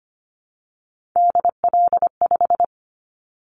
Morse code, heard as a keyed tone, about 25 wpm, 700 Hz. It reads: DL5
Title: Callsign Prefixes for Top 50 DXCC - 25wpm